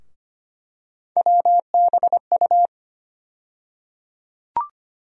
Morse code, heard as a keyed tone, about 25 wpm, 700 Hz. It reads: WBU
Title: USA Call Sign Postfixes - 3 Letters 25wpm